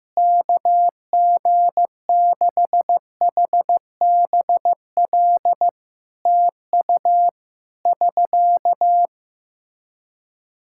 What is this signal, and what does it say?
Morse code, heard as a keyed tone, about 15 wpm, 700 Hz. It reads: KG6HBL TU <SK>
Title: Practice QSO - Exchange 4 - Ending - 15wpm